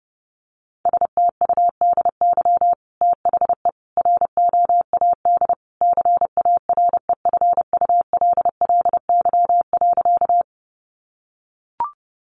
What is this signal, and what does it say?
Morse code, heard as a keyed tone, about 30 wpm, 700 Hz. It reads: STUDY THE ROAD CAREFULLY.
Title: Sentences from Top 1300 Words 30wpm